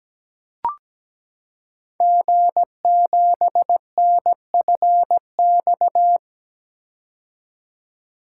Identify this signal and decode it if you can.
Morse code, heard as a keyed tone, about 17 wpm, 700 Hz. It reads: G7NFX
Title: Code Groups - 5 Letter:Numbers - 17wpm